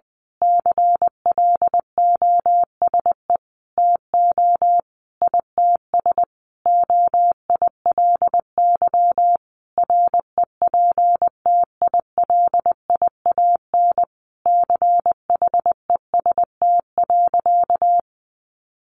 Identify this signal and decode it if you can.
Morse code, heard as a keyed tone, about 19 wpm, 700 Hz. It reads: CLOSE TO ITS OILY REPTILIAN CHEST.
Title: A Sound Of Thunder 20wpm